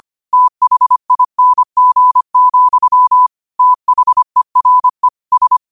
Morse code, heard as a keyed tone, about 25 wpm, 1000 Hz. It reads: THING, THERE S
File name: qotdfiles / 25wpm